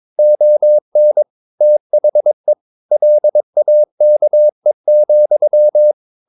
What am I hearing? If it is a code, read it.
Morse code, heard as a keyed tone, about 22 wpm, 600 Hz. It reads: ON THE LAKE,
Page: Romanen Never af Ken Follet på engelsk.